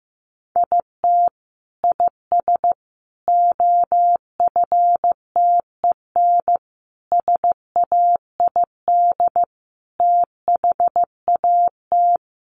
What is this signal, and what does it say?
Morse code, heard as a keyed tone, about 15 wpm, 700 Hz. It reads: IT IS OFTEN SAID THAT
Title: Sentences from Top 1400 Words 15wpm